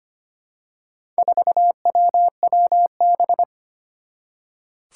Morse code, heard as a keyed tone, about 25 wpm, 700 Hz. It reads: 4WWB
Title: Four Challenging Letter-Number Combinations 25wpm